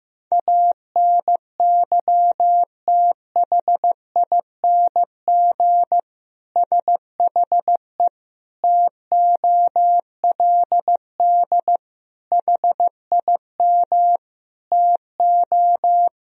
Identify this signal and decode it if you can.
Morse code, heard as a keyed tone, about 15 wpm, 700 Hz. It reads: ANYTHING SHE TOLD HIM TO
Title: Sentences from Top 1000 Words - Review Part 3 - 15wpm